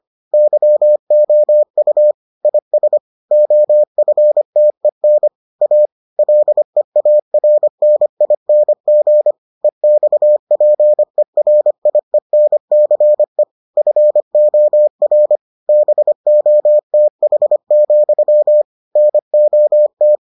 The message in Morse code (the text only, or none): YOU IS OFTEN A LEARNING EXPERIENCE FOR BOTH, NOT